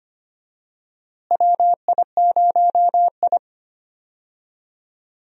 Morse code, heard as a keyed tone, about 25 wpm, 700 Hz. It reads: WI0I
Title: USA Call Signs - 4 Characters - Repeated 2x - 25wpm